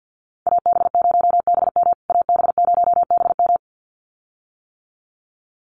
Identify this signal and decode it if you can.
Morse code, heard as a keyed tone, about 50 wpm, 700 Hz. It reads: AB0BM AB0BM
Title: USA Call Signs Repeated 2x 50wpm